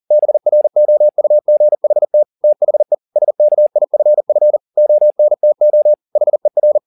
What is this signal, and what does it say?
Morse code, heard as a keyed tone, about 40 wpm, 600 Hz. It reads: BROUGHT THE SKIFF ONTO HER
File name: the old man and the sea 2 40wpm